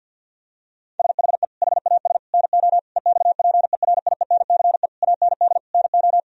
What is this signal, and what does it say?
Morse code, heard as a keyed tone, about 50 wpm, 700 Hz. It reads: SHE HAS NO EXPERIENCE AND NO
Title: Sentences from Top 1000 Words - Review Part 4 - 50wpm